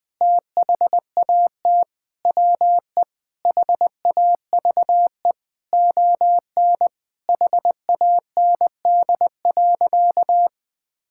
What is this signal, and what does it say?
Morse code, heard as a keyed tone, about 20 wpm, 700 Hz. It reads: THAT WE HAVE ON HAND.
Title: Sentences from Top 1000 Words - Review Part 6 - 20wpm